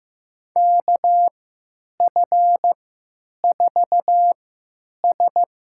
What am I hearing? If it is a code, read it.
Morse code, heard as a keyed tone, about 15 wpm, spaced as for 9 wpm, 700 Hz. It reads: KF4S
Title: USA Call Signs - 3x Character Spacing - 15wpm